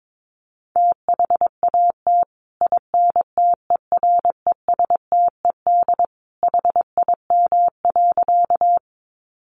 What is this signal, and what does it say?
Morse code, heard as a keyed tone, about 22 wpm, 700 Hz. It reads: THAT INTERESTED HIM.
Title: Sentences from Top 900 Words 22wpm